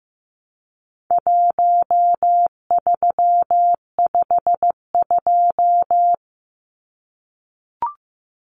Morse code, heard as a keyed tone, about 15 wpm, 700 Hz. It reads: 1352
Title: Numbers - Contest Style - 4-Digit 15wpm